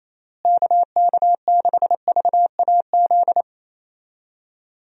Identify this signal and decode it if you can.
Morse code, heard as a keyed tone, about 27 wpm, 700 Hz. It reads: KK6VAZ
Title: USA Call Signs - 6 Characters - Repeated 3x - 28wpm